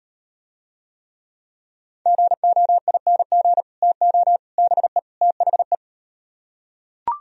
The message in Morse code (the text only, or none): GOING TO BE THE